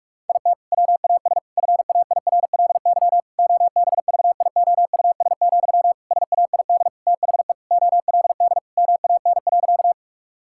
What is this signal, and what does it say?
Morse code, heard as a keyed tone, about 46 wpm, 700 Hz. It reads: IT WAS FAIRLY OBVIOUS, SAID THE OLD MAN.
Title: Sentences from Top 1600 Words 45wpm